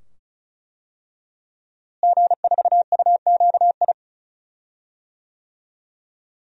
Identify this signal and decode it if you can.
Morse code, heard as a keyed tone, about 36 wpm, 700 Hz. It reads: G4UQI
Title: DXCC-100 Call Signs - 5 Characters - Repeated 2x - 35wpm